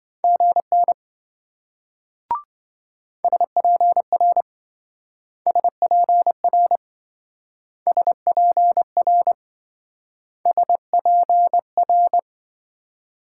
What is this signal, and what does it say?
Morse code, heard as a keyed tone, about 26 wpm, 700 Hz. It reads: GN SPR SPR SPR SPR